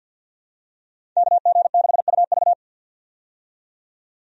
Morse code, heard as a keyed tone, about 52 wpm, 700 Hz. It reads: KG6UV
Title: USA Call Signs - 5 Characters - 50wpm